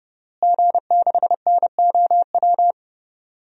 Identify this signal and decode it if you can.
Morse code, heard as a keyed tone, about 30 wpm, 700 Hz. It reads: G6NOW
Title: DXCC-100 Call Signs - 5 Characters - Repeated 3x - 30wpm